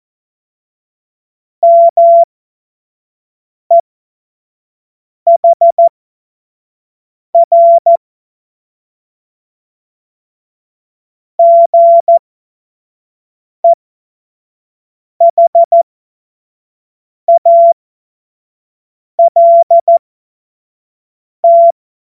Morse code, heard as a keyed tone, about 14 wpm, spaced as for 5 wpm, 680 Hz. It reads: MEHR GEHALT
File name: Morsekurs / Witze / Witz 18. Juli 2023 25 F70